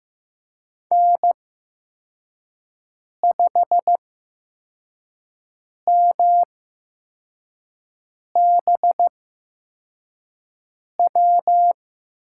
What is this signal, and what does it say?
Morse code, heard as a keyed tone, about 15 wpm, spaced as for 4 wpm, 700 Hz. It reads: N5MBW
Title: USA Call Signs - 8x Character Spacing - 15wpm